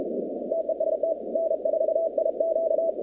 Morse code, heard as a keyed tone, about 32 wpm, 650 Hz, with noise. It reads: TEST N4IQ
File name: contests / 2019-cw-wpx-kt5j-k5pi- -2330 0-47